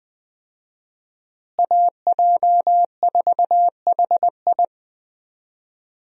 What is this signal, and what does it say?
Morse code, heard as a keyed tone, about 20 wpm, 700 Hz. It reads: AJ4HI
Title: USA Call Signs - 5 Characters - Repeated 3x - 20wpm